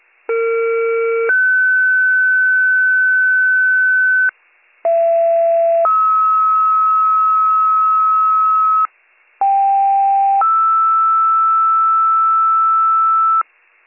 Two-tone page, heard as a one-second tone, then a three-second tone, about 460 then 1500 Hz, about 670 then 1290 Hz, about 790 then 1320 Hz, with noise. It is heard from 0.3 to 13.4 s.